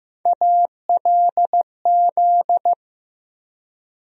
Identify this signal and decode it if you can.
Morse code, heard as a keyed tone, about 15 wpm, 700 Hz. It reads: ALZ
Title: USA Call Sign Postfixes - 3 Letters 15wpm